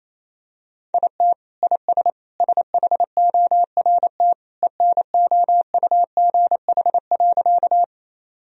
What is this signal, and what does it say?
Morse code, heard as a keyed tone, about 27 wpm, 700 Hz. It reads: IT IS SHORT ENOUGH.
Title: Sentences from Top 1000 Words - Review Part 2 - 28wpm